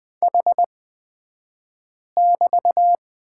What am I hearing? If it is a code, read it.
Morse code, heard as a keyed tone, about 20 wpm, spaced as for 5 wpm, 700 Hz. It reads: H<BT>
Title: 235 Lesson 38d - Letter q - Two Characters 20wpm